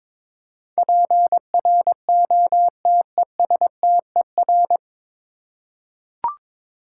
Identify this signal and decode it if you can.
Morse code, heard as a keyed tone, about 22 wpm, 700 Hz. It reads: PROTESTER
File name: 9 Letter Words - First 3 Letters Given - 22wpm